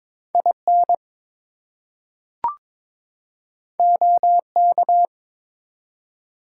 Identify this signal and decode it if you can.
Morse code, heard as a keyed tone, about 22 wpm, 700 Hz. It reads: IN OK
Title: Two Letter Words - 22wpm